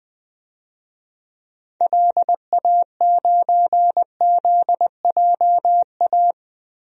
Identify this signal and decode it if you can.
Morse code, heard as a keyed tone, about 21 wpm, 700 Hz. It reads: LA9ZJA
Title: DXCC-100 Call Signs - 6 Characters - 20wpm